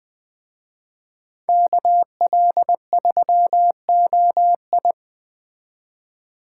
Morse code, heard as a keyed tone, about 21 wpm, 700 Hz. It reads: KL3OI
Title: Speed Racing - USA 2x2 Call Signs 17wpm